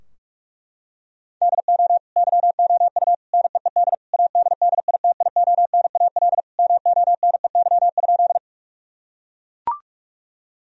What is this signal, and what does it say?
Morse code, heard as a keyed tone, about 41 wpm, 700 Hz. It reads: DO YOU NEED ADDITIONAL MONEY?
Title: Sentences from Top 1000 Words - Review Part 4 - 45wpm